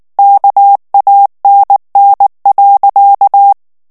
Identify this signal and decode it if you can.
Morse code, heard as a keyed tone, about 19 wpm, 800 Hz. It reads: KANN.
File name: morse mp3 / ALLEMAND texte clair 2 vitesse 18 mots minute